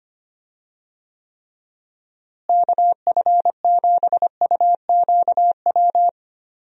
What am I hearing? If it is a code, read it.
Morse code, heard as a keyed tone, about 25 wpm, 700 Hz. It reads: KF7UQW
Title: USA Call Signs - 6 Characters - Repeated 2x - 25wpm